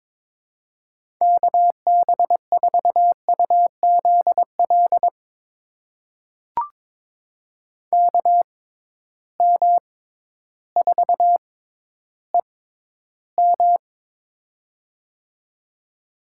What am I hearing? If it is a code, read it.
Morse code, heard as a keyed tone, about 22 wpm, spaced as for 12 wpm, 700 Hz. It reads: KB4UZL KM4EM
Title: USA Call Signs - 6x Character Spacing - 22wpm